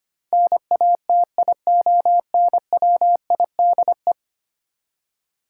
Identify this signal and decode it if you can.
Morse code, heard as a keyed tone, about 25 wpm, 700 Hz. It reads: NATIONWIDE
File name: Ten Letter Words - 25wpm